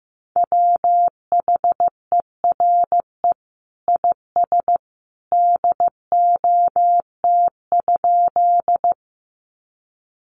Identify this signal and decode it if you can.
Morse code, heard as a keyed tone, about 15 wpm, 700 Hz. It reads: WHERE IS DOT?
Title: Sentences from Top 1500 Words 15wpm